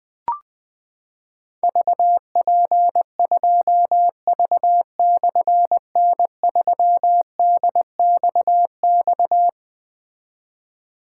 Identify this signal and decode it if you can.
Morse code, heard as a keyed tone, about 20 wpm, 700 Hz. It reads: VP2V/N3DXX
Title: Difficult Call Signs 20wpm